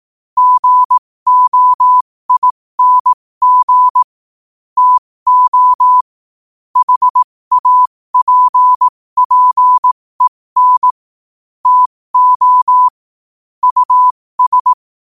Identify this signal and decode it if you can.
Morse code, heard as a keyed tone, about 18 wpm, spaced as for 15 wpm, 1000 Hz. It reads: GOING TO HAPPEN TO US
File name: qotdfiles / 15wpm